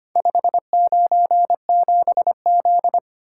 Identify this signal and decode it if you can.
Morse code, heard as a keyed tone, about 25 wpm, 690 Hz. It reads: 597Z
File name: practice / 25wpm / Four Letter-Number Combinations - 25wpm